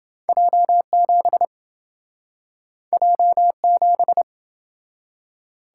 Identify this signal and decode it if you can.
Morse code, heard as a keyed tone, about 29 wpm, 700 Hz. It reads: J7 J7